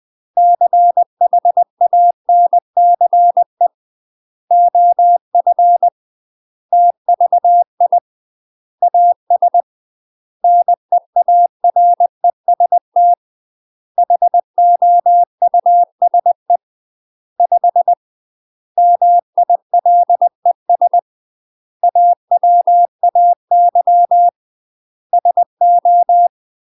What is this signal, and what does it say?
Morse code, heard as a keyed tone, about 20 wpm, 700 Hz. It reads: CHANCE OF TVI AS NEAREST HOUSE 5 MILES AWAY SO
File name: UK MASTER QSOs TRAINING / 20WPM / UK MASTER QSOs 0030